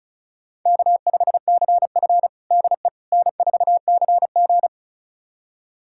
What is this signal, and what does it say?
Morse code, heard as a keyed tone, about 36 wpm, 700 Hz. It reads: K5CF DE N4CG
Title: USA Callsign DE Callsign - 4 Characters - 35wpm